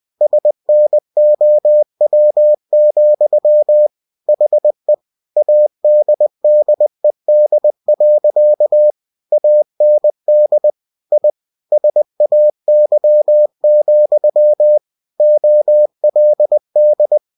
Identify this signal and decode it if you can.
Morse code, heard as a keyed tone, about 20 wpm, 600 Hz. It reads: SNOW, HE ADDED. AND I SAY, OLD
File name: cw-books / winnie-the-pooh / winnie-20wpm-page-0019